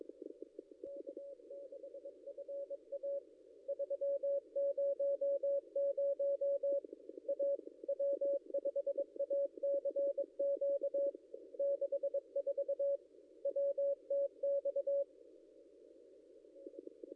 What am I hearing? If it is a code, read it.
Morse code, heard as a keyed tone, about 22 wpm, 560 Hz. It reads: K6FA 300 A W5ACQ 64 WTX
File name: contests / ss-cw-k5tr-op-n5rz -0800